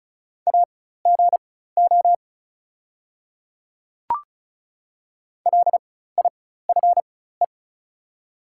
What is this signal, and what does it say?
Morse code, heard as a keyed tone, about 34 wpm, spaced as for 16 wpm, 690 Hz. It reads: AGO LIFE
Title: 4x Character Spacing - Top 500 Words - 35wpm